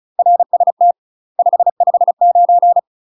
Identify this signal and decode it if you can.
Morse code, heard as a keyed tone, about 34 wpm, 700 Hz. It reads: RST 559
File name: UK MASTER QSOs TRAINING / 35WPM / UK MASTER QSOs 0039